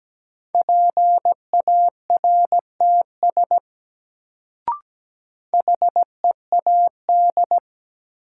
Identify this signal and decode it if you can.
Morse code, heard as a keyed tone, about 17 wpm, 700 Hz. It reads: PARTS HEAD
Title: Sets of 3 Words - New Word Drill - 17wpm